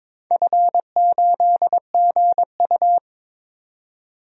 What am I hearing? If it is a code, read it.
Morse code, heard as a keyed tone, about 22 wpm, 700 Hz. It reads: F8GU
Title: DXCC-100 Call Signs - 4 Characters - 22wpm